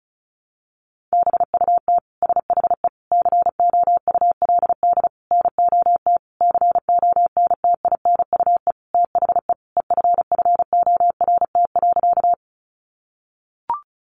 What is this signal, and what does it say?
Morse code, heard as a keyed tone, about 34 wpm, 700 Hz. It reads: BUT SHE COULD NOT CONTINUE THE EFFORT.
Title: Sentences from Top 1200 Words 35wpm